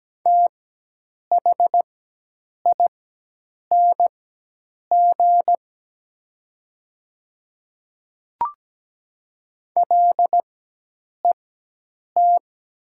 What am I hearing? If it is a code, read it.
Morse code, heard as a keyed tone, about 17 wpm, spaced as for 8 wpm, 700 Hz. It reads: THING LET
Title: 4x Character Spacing - Top 500 Words - 17wpm